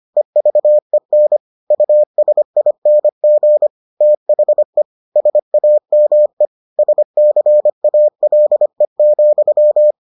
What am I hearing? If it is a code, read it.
Morse code, heard as a keyed tone, about 25 wpm, 600 Hz. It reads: EVEN USING THE SAME SCALE,